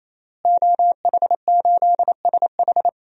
Morse code, heard as a keyed tone, about 27 wpm, 700 Hz. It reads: OH8SH
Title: DXCC-100 Call Signs - 5 Characters - Repeated 3x - 28wpm